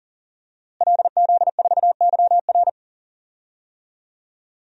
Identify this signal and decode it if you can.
Morse code, heard as a keyed tone, about 40 wpm, 700 Hz. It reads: LZ4YR
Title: DXCC-100 Call Signs - 5 Characters - Repeated 3x - 40wpm